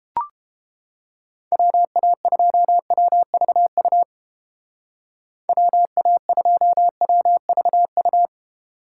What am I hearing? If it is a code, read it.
Morse code, heard as a keyed tone, about 29 wpm, 700 Hz. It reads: WA2WVU WA2WVU